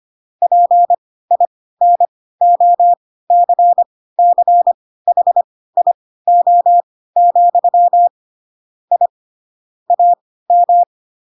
Morse code, heard as a keyed tone, about 25 wpm, spaced as for 16 wpm, 700 Hz. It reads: PINOCCHIO, I AM
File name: Pinocchio / Pinocchio ch4 10 25 ik1hge